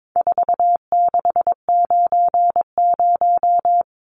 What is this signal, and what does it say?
Morse code, heard as a keyed tone, about 22 wpm, 700 Hz. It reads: 4690
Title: Numbers - 4-Digit 22wpm